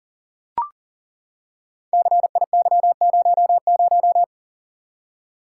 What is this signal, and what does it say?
Morse code, heard as a keyed tone, about 38 wpm, 690 Hz. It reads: CIY00
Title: Code Groups - 5 Letter:Numbers - 40wpm